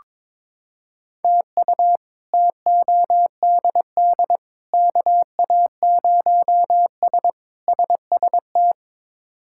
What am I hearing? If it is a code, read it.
Morse code, heard as a keyed tone, about 22 wpm, 700 Hz. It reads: TU TODD KA0S SST
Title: K1USN SST QSO Practice - 22wpm